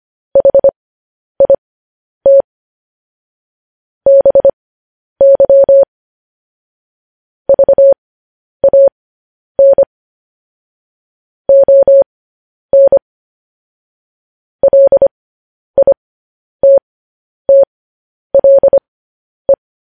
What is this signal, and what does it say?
Morse code, heard as a keyed tone, about 25 wpm, spaced as for 10 wpm, 550 Hz. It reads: HIT BY VAN ON LITTLE
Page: News Headlines in Morse Code at 10 WPM